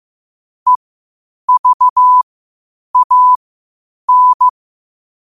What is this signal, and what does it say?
Morse code, heard as a keyed tone, about 15 wpm, spaced as for 8 wpm, 1000 Hz. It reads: EVAN
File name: qotdfiles / 8wpm